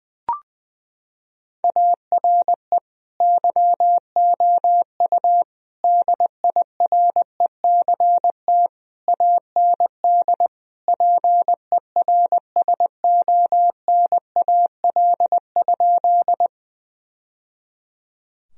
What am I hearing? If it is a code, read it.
Morse code, heard as a keyed tone, about 21 wpm, 700 Hz. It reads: ARE YOU DIRECT AND PERSONAL?
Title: Sentences from Top 1000 Words - Review Part 4 - 20wpm